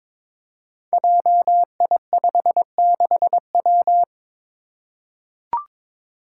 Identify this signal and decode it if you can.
Morse code, heard as a keyed tone, about 23 wpm, 700 Hz. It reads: JI56W
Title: Five Challenging Letter-Number Combinations 22wpm